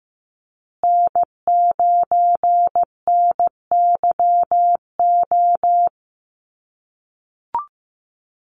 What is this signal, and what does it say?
Morse code, heard as a keyed tone, about 15 wpm, 700 Hz. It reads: N9NYO
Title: USA Call Signs Repeated 2x 15wpm